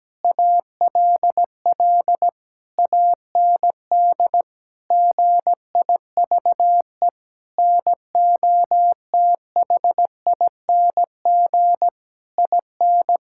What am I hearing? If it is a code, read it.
Morse code, heard as a keyed tone, about 17 wpm, 700 Hz. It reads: ALL AND GIVE NOTHING IN
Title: Sentences from Top 1000 Words - Review Part 6 - 17wpm